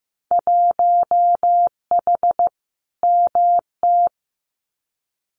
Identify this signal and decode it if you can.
Morse code, heard as a keyed tone, about 15 wpm, 700 Hz. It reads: 1H MT
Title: Winter Field Day Exchanges Encore 15wpm